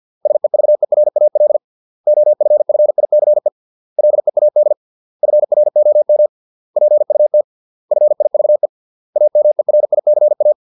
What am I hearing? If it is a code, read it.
Morse code, heard as a keyed tone, about 50 wpm, 600 Hz. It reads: SEVERAL OFFICE LEAD FROM PUT FIVE AMERICA